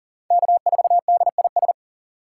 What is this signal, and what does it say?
Morse code, heard as a keyed tone, about 40 wpm, 700 Hz. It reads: K4DIS
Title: USA Call Signs - 8x Character Spacing - 40wpm